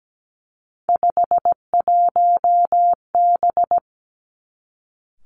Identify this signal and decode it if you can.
Morse code, heard as a keyed tone, about 17 wpm, 700 Hz. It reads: 51B
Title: Three Challenging Letter-Number Combinations 17wpm